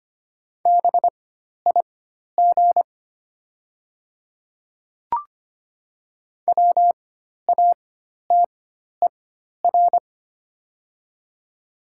Morse code, heard as a keyed tone, about 25 wpm, spaced as for 12 wpm, 700 Hz. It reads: BIG WATER
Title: 4x Character Spacing - Top 300 Words - 25wpm